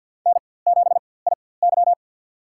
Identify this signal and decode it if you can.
Morse code, heard as a keyed tone, about 49 wpm, spaced as for 23 wpm, 690 Hz. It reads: N7IY
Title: USA Call Signs - 4x Character Spacing - 50wpm